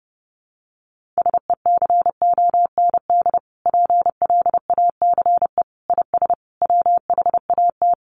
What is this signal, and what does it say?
Morse code, heard as a keyed tone, about 30 wpm, 700 Hz. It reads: SECOND PLACE IS WHAT
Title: Sentences from Top 1500 Words 30wpm